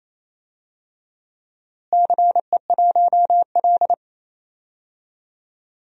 Morse code, heard as a keyed tone, about 28 wpm, 700 Hz. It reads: CE1L
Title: DXCC-100 Call Signs - 4 Characters - Repeated 2x - 28wpm